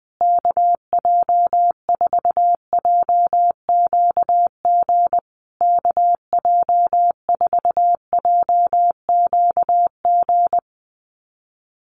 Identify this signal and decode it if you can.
Morse code, heard as a keyed tone, about 20 wpm, 700 Hz. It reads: KJ4JQG KJ4JQG
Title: USA Call Signs Repeated 3x 20wpm